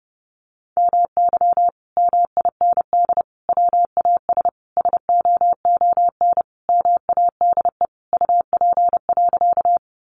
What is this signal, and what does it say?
Morse code, heard as a keyed tone, about 30 wpm, 700 Hz. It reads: MY MIND WAS SOON MADE UP.
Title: Sentences from Top 500 Words 30wpm